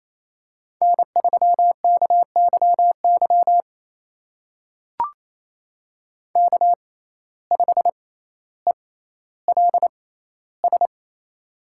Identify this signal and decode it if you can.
Morse code, heard as a keyed tone, about 28 wpm, spaced as for 14 wpm, 700 Hz. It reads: N3KYY K5ELS
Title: USA Call Signs - 5 Characters - 6x Character Spacing - 28wpm